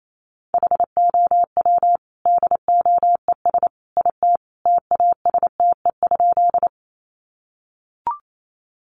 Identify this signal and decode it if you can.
Morse code, heard as a keyed tone, about 28 wpm, 700 Hz. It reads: HOW DOES IT TASTE?
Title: Sentences from Top 1100 Words 28wpm